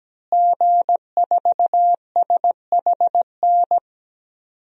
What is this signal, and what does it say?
Morse code, heard as a keyed tone, about 17 wpm, 700 Hz. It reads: G4SHN
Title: DXCC-100 Call Signs - 5 Characters - Repeated 2x - 17wpm